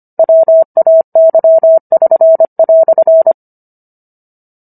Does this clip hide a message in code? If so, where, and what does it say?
Yes: Morse code from 0.2 to 3.3 s, heard as a keyed tone, about 25 wpm, 650 Hz. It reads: WAY<SN>"